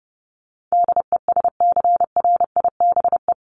Morse code, heard as a keyed tone, about 30 wpm, 700 Hz. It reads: DESCRIBE
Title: ED Suffix - Top 100 Words with Base Words 30wpm